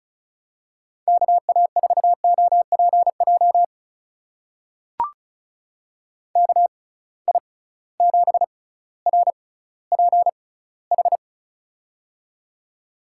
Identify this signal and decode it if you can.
Morse code, heard as a keyed tone, about 36 wpm, spaced as for 18 wpm, 700 Hz. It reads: KA4OPJ KI7RPH
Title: USA Call Signs - 6 Characters - 6x Character Spacing - 35wpm